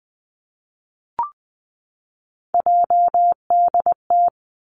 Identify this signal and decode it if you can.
Morse code, heard as a keyed tone, about 20 wpm, 700 Hz. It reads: JDT
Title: Three Challenging Letter-Number Combinations 20wpm